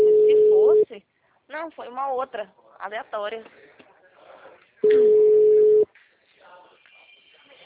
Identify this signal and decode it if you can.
Ringback tone (Europe, 425 Hz), with speech.
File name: out- -2015- - - .6200